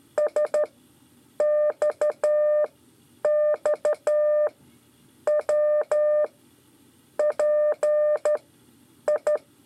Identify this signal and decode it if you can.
Morse code, heard as a keyed tone, about 12 wpm, spaced as for 8 wpm, 600 Hz. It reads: SXXW PI
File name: 5 LEZIONE RIPASSO 1